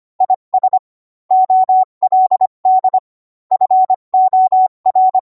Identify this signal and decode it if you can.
Morse code, heard as a keyed tone, about 25 wpm, 750 Hz. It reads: IS OLD FOR